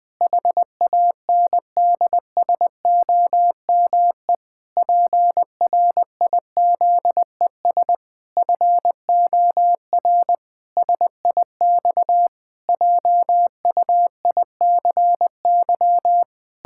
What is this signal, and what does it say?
Morse code, heard as a keyed tone, about 21 wpm, 700 Hz. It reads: HANDSOME PRIZES FOR SIX JUICY
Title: Pangram Challenge v2 - 20wpm